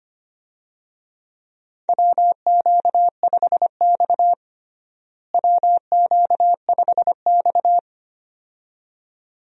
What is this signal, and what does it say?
Morse code, heard as a keyed tone, about 25 wpm, 700 Hz. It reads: WQ5X WQ5X
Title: USA Call Signs - 4 Characters - Repeated 2x - 25wpm